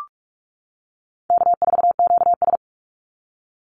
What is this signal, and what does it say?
Morse code, heard as a keyed tone, about 44 wpm, 700 Hz. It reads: K4QS
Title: NA SOTA Activator Callsigns 45wpm